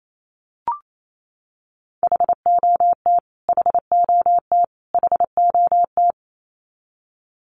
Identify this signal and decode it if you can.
Morse code, heard as a keyed tone, about 28 wpm, 700 Hz. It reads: HOT HOT HOT
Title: Top 500 Words Repeated 28wpm